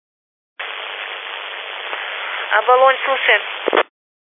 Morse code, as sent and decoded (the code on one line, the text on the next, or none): none